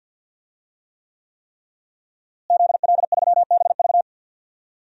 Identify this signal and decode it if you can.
Morse code, heard as a keyed tone, about 52 wpm, 700 Hz. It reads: ZL3BV
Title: DXCC-100 Call Signs - 5 Characters - Repeated 3x - 50wpm